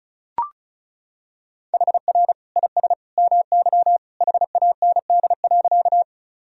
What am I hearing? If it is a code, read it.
Morse code, heard as a keyed tone, about 36 wpm, 690 Hz. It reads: HR IS MY HAND.
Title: Sentences from Top 200 Words and Abbreviations 35wpm